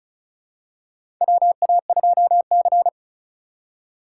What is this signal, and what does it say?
Morse code, heard as a keyed tone, about 36 wpm, 700 Hz. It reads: WA2C
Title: USA Call Signs - 4 Characters - 3x Character Spacing - 35wpm